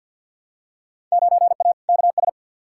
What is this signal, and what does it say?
Morse code, heard as a keyed tone, about 48 wpm, 700 Hz. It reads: 9A KS
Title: Field Day Exchanges Encore - 50wpm